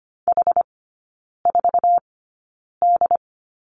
Morse code, heard as a keyed tone, about 25 wpm, spaced as for 9 wpm, 700 Hz. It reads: H4D
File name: course / 25:10wpm / 093 Lesson 16e - Letter c - Three Characters 25wpm